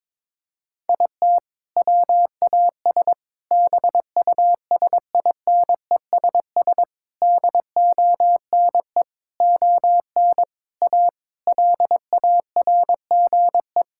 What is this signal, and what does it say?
Morse code, heard as a keyed tone, about 22 wpm, 700 Hz. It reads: IT WAS BUSINESS DONE ON A LARGE
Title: Sentences from Top 1000 Words - Review Part 5 - 22wpm